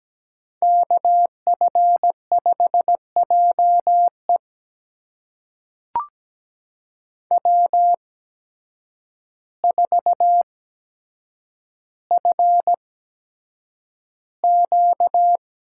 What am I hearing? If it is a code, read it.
Morse code, heard as a keyed tone, about 17 wpm, spaced as for 8 wpm, 700 Hz. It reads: KF5JE W4FQ
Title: USA Call Signs - 5 Characters - 8x Character Spacing - 17wpm